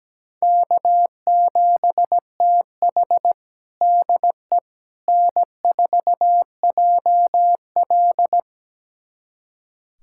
Morse code, heard as a keyed tone, about 17 wpm, 700 Hz. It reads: K7TH DE N4JL
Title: USA Callsign DE Callsign - 4 Characters - 17wpm